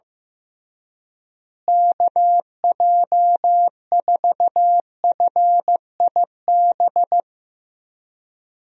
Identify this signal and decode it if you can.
Morse code, heard as a keyed tone, about 15 wpm, 700 Hz. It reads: KJ4FIB
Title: USA Call Signs - 6 Characters - Repeated 3x - 15wpm